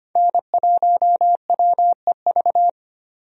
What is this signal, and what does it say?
Morse code, heard as a keyed tone, about 25 wpm, 700 Hz. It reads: N1WEV
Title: USA Call Signs - 5 Characters - 25wpm